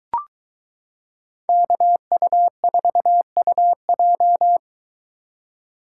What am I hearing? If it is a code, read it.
Morse code, heard as a keyed tone, about 23 wpm, 700 Hz. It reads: KU4UJ